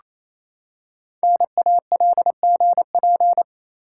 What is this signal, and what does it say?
Morse code, heard as a keyed tone, about 28 wpm, 690 Hz. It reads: NALGP
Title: Code Groups - 5 Letters - 28wpm